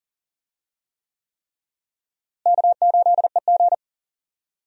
Morse code, heard as a keyed tone, about 40 wpm, 700 Hz. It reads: K8EG
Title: USA Call Signs - 4 Characters - Repeated 3x - 40wpm